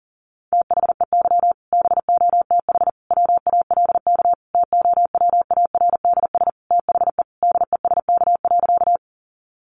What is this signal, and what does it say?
Morse code, heard as a keyed tone, about 40 wpm, 700 Hz. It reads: THEY BOTH WALK TOWARDS THE DESK.
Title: Additional Easy Phrases 40wpm